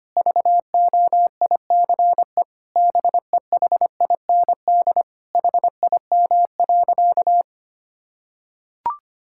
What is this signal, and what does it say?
Morse code, heard as a keyed tone, about 25 wpm, 700 Hz. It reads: VOICE BEHIND HIM.
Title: Sentences from Top 1600 Words 25wpm